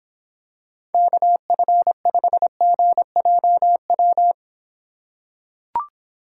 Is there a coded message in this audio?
Yes: Morse code from 0.9 to 4.3 s, heard as a keyed tone, about 26 wpm, 700 Hz. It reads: KF5GJW